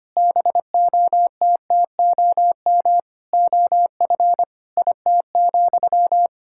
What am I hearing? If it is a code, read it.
Morse code, heard as a keyed tone, about 25 wpm, 700 Hz. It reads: BOTTOM OF IT,
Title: Sentences from Top 1000 Words - Review Part 6 - 25wpm